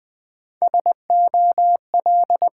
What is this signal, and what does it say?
Morse code, heard as a keyed tone, about 20 wpm, 700 Hz. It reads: SOL
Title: Build A Word - 1 Letter At A Time - 5-Letter Words - 20wpm